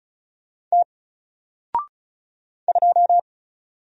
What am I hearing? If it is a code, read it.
Morse code, heard as a keyed tone, about 34 wpm, 700 Hz. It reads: T 2
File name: Single-Letter-Number-35wpm